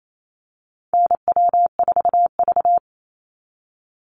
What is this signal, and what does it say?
Morse code, heard as a keyed tone, about 28 wpm, 700 Hz. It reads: NW4V
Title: Call Signs Repeated 3x - 28wpm